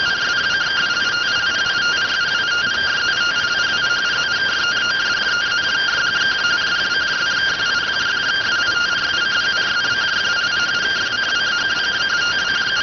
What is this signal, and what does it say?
Two-tone teleprinter signal, about 170 Hz shift.